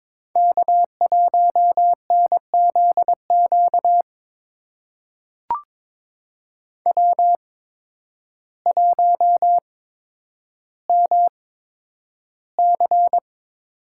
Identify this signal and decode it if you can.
Morse code, heard as a keyed tone, about 22 wpm, spaced as for 10 wpm, 700 Hz. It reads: K1NZQ W1MC
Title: USA Call Signs - 5 Characters - 8x Character Spacing - 22wpm